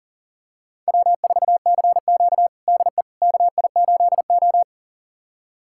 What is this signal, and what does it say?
Morse code, heard as a keyed tone, about 40 wpm, 700 Hz. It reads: W4CQ DE KI8O
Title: USA Callsign DE Callsign - 4 Characters - 40wpm